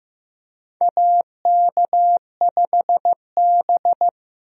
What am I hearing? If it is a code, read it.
Morse code, heard as a keyed tone, about 15 wpm, 700 Hz. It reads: AK5B
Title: USA Call Signs - 4 Characters - Repeated 2x - 15wpm